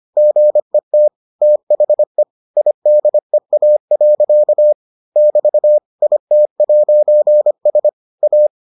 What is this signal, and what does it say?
Morse code, heard as a keyed tone, about 25 wpm, 600 Hz. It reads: GET THE IDEA. <BT> IT'S A